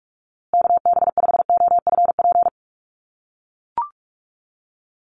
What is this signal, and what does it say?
Morse code, heard as a keyed tone, about 44 wpm, 700 Hz. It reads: KB5OFP
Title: USA Call Signs Repeated 2x 45wpm